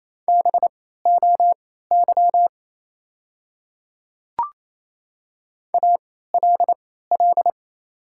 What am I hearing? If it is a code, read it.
Morse code, heard as a keyed tone, about 28 wpm, spaced as for 16 wpm, 700 Hz. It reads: BOY ALL
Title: 3x Character Spacing - Top 300 Words - 28wpm